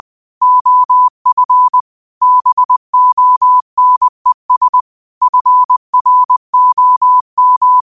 Morse code, heard as a keyed tone, about 20 wpm, 1000 Hz. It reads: OF BONES FROM